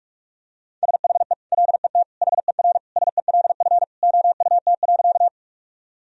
Morse code, heard as a keyed tone, about 47 wpm, 700 Hz. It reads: SHE LET HER SELF OUT.
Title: Sentences from Top 1000 Words - Review Part 1 - 45wpm